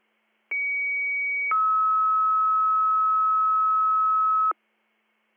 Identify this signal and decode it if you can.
Two-tone page, heard as a one-second tone, then a three-second tone, about 2250 then 1290 Hz. It is heard from 0.5 to 4.5 s.